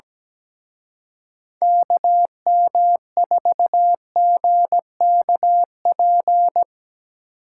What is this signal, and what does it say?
Morse code, heard as a keyed tone, about 20 wpm, 700 Hz. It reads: KM4GKP
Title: Speed Racing - USA Call Signs 17wpm